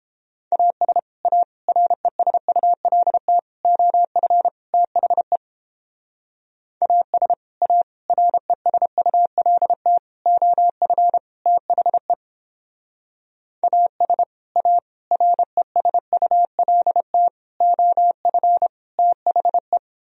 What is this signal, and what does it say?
Morse code, heard as a keyed tone, about 29 wpm, 700 Hz. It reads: AS A RESULT OF THE AS A RESULT OF THE AS A RESULT OF THE